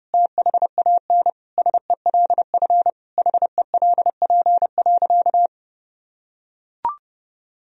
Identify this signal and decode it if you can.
Morse code, heard as a keyed tone, about 32 wpm, 700 Hz. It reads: THAN SELF HELP.
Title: Sentences from Top 1000 Words - Review Part 1 - 30wpm